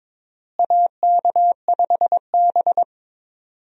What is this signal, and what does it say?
Morse code, heard as a keyed tone, about 22 wpm, 700 Hz. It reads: AK5B
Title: USA Call Signs - 4 Characters - Repeated 2x - 22wpm